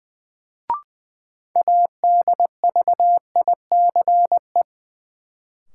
Morse code, heard as a keyed tone, about 20 wpm, 700 Hz. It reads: ADVICE
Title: Sentences from Top 1100 Words - New Word Drill - 20wpm